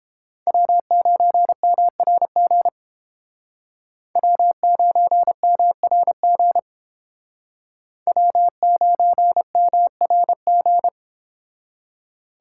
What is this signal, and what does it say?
Morse code, heard as a keyed tone, about 29 wpm, 700 Hz. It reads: W9MRG W9MRG W9MRG